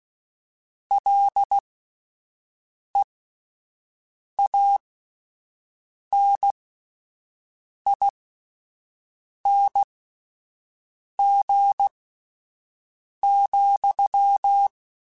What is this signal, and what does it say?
Morse code, heard as a keyed tone, about 16 wpm, spaced as for 5 wpm, 800 Hz. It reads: LEANING,